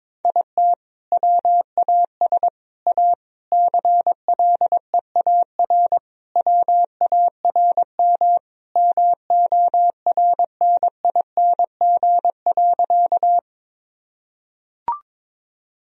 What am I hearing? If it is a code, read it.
Morse code, heard as a keyed tone, about 21 wpm, 700 Hz. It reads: IT WAS A CLEAR WARM MORNING.
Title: Sentences from Top 1000 Words - Review Part 2 - 22wpm